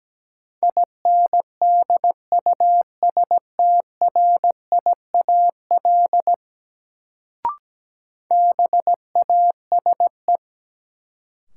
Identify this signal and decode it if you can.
Morse code, heard as a keyed tone, about 17 wpm, 700 Hz. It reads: INDUSTRIAL BASE
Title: Sets of 5 Words - New Word Drill - 17wpm